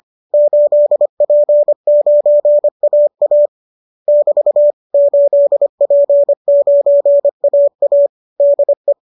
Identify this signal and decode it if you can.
Morse code, heard as a keyed tone, about 25 wpm, 570 Hz. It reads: 8P9AA <BT> 8P9AA DE